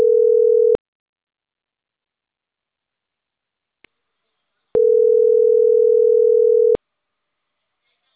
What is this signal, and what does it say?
Ringback tone (North America), 440 + 480 Hz, with speech.
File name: CENTRAL CAEA / out agcaea01 - .26959